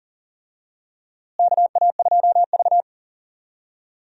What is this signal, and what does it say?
Morse code, heard as a keyed tone, about 40 wpm, 700 Hz. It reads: KA2V
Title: USA Call Signs - 4 Characters - Repeated 3x - 40wpm